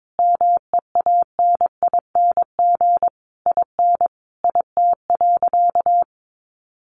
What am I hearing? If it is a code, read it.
Morse code, heard as a keyed tone, about 22 wpm, 700 Hz. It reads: MEANING IN IT.
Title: Sentences from Top 1200 Words 22wpm